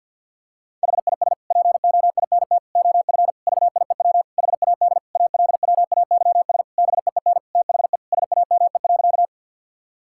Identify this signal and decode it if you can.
Morse code, heard as a keyed tone, about 50 wpm, 700 Hz. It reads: HIS POINT OF VIEW HAD ALWAYS BEEN THE SAME.
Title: Sentences from Top 1000 Words - Review Part 6 - 50wpm